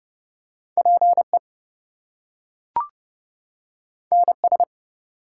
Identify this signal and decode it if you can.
Morse code, heard as a keyed tone, about 28 wpm, 700 Hz. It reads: PE NS
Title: US States and Canadian Provinces 30wpm